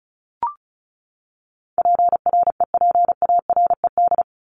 Morse code, heard as a keyed tone, about 34 wpm, 700 Hz. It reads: PREPARED
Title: ED Suffix - Top 100 Words with Base Words 35wpm